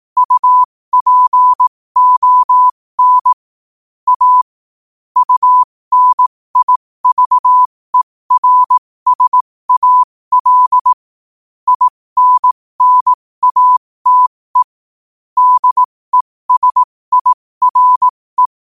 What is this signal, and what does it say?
Morse code, heard as a keyed tone, about 18 wpm, spaced as for 15 wpm, 1000 Hz. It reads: UPON A UNIVERSAL INNATE DESIRE
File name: qotdfiles / 15wpm